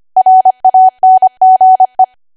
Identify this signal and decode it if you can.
Morse code, heard as a keyed tone, about 25 wpm, 750 Hz. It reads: RANGE
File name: cwa / W1AW-Level3 / 25WPM